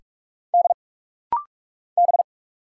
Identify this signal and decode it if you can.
Morse code, heard as a keyed tone, about 45 wpm, 700 Hz. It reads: D B
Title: Single Letter-Number Alt - Rapid-Fire 45wpm